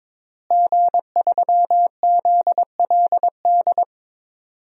Morse code, heard as a keyed tone, about 22 wpm, 700 Hz. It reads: G3ZLD
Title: DXCC-100 Call Signs - 5 Characters - Repeated 3x - 22wpm